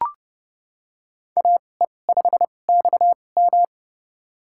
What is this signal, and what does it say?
Morse code, heard as a keyed tone, about 32 wpm, spaced as for 22 wpm, 700 Hz. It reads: AE5XM
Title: USA Call Signs - 2x Character Spacing - 30wpm